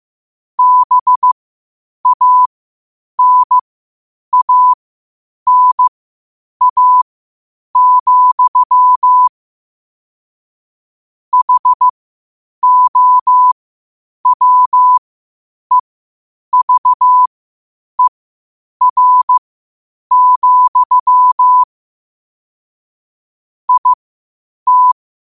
Morse code, heard as a keyed tone, about 15 wpm, spaced as for 8 wpm, 1000 Hz. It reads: BANANA, HOWEVER, IT
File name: qotdfiles / 8wpm